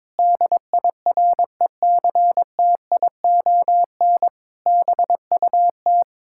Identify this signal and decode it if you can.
Morse code, heard as a keyed tone, about 22 wpm, 700 Hz. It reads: DIRECTION BUT
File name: practice / 22wpm / Sets of 8 Words - 22wpm